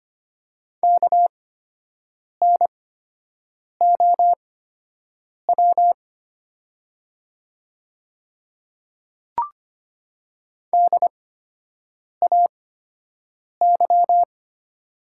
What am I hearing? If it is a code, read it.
Morse code, heard as a keyed tone, about 25 wpm, spaced as for 7 wpm, 690 Hz. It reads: KNOW DAY
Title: 8x Character Spacing - Top 100 Words - 25wpm